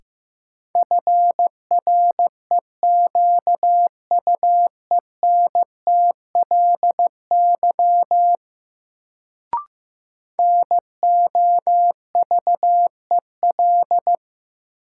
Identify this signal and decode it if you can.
Morse code, heard as a keyed tone, about 15 wpm, 700 Hz. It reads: FREQUENTLY NOVEL
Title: Sentences from Top 1300 Words - New Word Drill - 15wpm